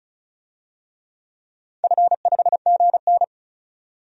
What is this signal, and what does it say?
Morse code, heard as a keyed tone, about 36 wpm, 700 Hz. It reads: F5GN
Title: DXCC-100 Call Signs - 4 Characters - 35wpm